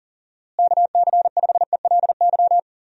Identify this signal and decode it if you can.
Morse code, heard as a keyed tone, about 40 wpm, 700 Hz. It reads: KC5ELY
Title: USA Call Signs - 6 Characters - Repeated 2x - 40wpm